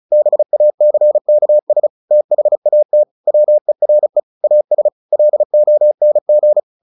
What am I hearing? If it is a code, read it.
Morse code, heard as a keyed tone, about 34 wpm, 600 Hz. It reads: BACKS THAT WERE AS LONG
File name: the old man and the sea / the old man and the sea 1 35wpm